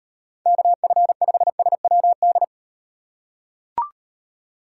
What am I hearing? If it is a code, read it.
Morse code, heard as a keyed tone, about 38 wpm, 700 Hz. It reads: KF5SWD